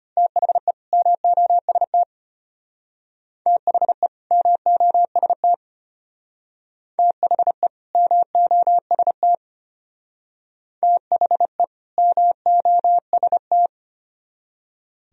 Morse code, heard as a keyed tone, about 32 wpm, 700 Hz. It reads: THE MOST THE MOST THE MOST THE MOST